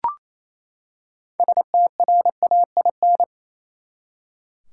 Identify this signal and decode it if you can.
Morse code, heard as a keyed tone, about 28 wpm, 700 Hz. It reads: STRAIN
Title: Six Letter Words - 28wpm